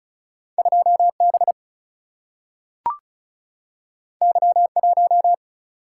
Morse code, heard as a keyed tone, about 36 wpm, 700 Hz. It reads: 2 B Y 1
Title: Single-Letter Single-Number 35wpm